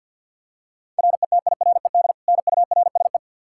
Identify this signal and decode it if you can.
Morse code, heard as a keyed tone, about 52 wpm, 700 Hz. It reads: RETIRED NURSE
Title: Practice QSO - Exchange 3 - More Info - 50wpm